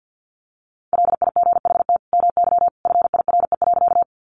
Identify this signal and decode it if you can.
Morse code, heard as a keyed tone, about 50 wpm, 700 Hz. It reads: LIGHT MY FIRE.
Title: Sentences from Top 400 Words 50wpm